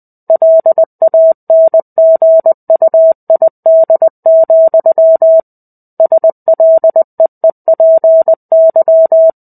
Morse code, heard as a keyed tone, about 20 wpm, 650 Hz. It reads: LANGUID, SLEEPY
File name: 05-Chapter-05 20 WPM